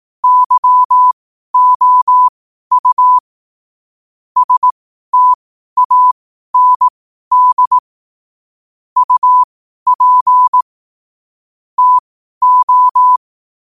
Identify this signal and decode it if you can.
Morse code, heard as a keyed tone, about 18 wpm, spaced as for 12 wpm, 1000 Hz. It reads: YOU STAND UP TO
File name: qotdfiles / 12wpm